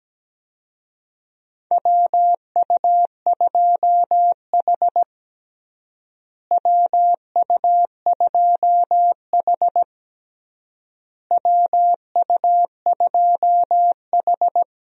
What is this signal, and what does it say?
Morse code, heard as a keyed tone, about 17 wpm, 700 Hz. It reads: WU2H WU2H WU2H
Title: USA Call Signs - 4 Characters - Repeated 3x - 17wpm